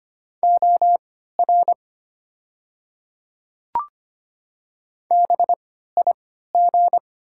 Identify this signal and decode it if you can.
Morse code, heard as a keyed tone, about 25 wpm, spaced as for 14 wpm, 700 Hz. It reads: OR BIG
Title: 3x Character Spacing - Top 300 Words - 25wpm